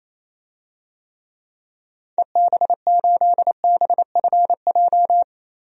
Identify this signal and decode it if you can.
Morse code, heard as a keyed tone, about 28 wpm, 700 Hz. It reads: EB8BFJ
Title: DXCC-100 Call Signs - 6 Characters - 28wpm